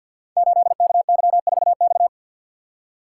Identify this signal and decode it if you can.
Morse code, heard as a keyed tone, about 51 wpm, 700 Hz. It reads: 8KY4X
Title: Code Groups - 5 Letter:Numbers - 50wpm